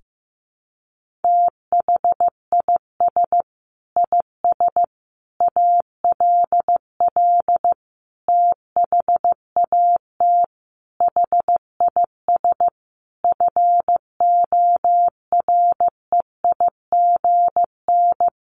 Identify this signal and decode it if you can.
Morse code, heard as a keyed tone, about 15 wpm, 700 Hz. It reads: THIS IS ALL THAT HIS FOREIGN
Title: Sentences from Top 1400 Words 15wpm